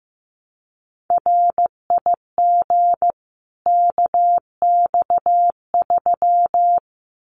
Morse code, heard as a keyed tone, about 15 wpm, 700 Hz. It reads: RIG KX3
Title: Common QSO Phrases 15wpm